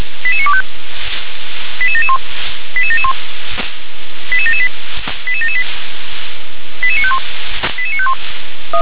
Five-tone selective call (ZVEI), heard as quick runs of five tones, tones of about 70 ms. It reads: 88925 88771 88771 88778 88778 88951 88951
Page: SDR/ZVEI